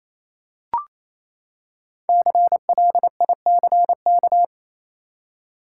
Morse code, heard as a keyed tone, about 28 wpm, 700 Hz. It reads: CLICK
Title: Five Letter Words - 28wpm